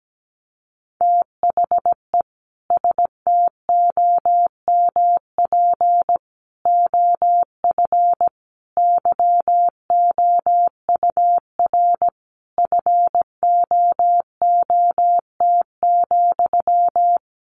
Morse code, heard as a keyed tone, about 17 wpm, 700 Hz. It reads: THE STOMP OF YOUR FOOT,
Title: A Sound Of Thunder 17wpm